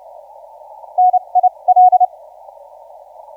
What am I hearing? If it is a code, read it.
Morse code, heard as a keyed tone, about 29 wpm, spaced as for 24 wpm, 730 Hz. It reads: NIL
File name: VP2EAQ 0350UTC 20m